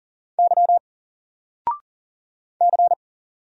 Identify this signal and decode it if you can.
Morse code, heard as a keyed tone, about 40 wpm, 700 Hz. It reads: Y C
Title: Single Letter-Number 40wpm